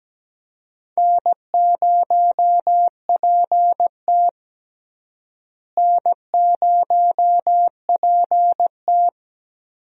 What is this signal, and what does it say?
Morse code, heard as a keyed tone, about 17 wpm, 700 Hz. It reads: N0PT N0PT
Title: USA Call Signs - 4 Characters - Repeated 3x - 17wpm